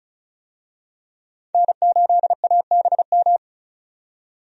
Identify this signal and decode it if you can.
Morse code, heard as a keyed tone, about 36 wpm, 700 Hz. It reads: N8ABM
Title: USA Call Signs - 5 Characters - 35wpm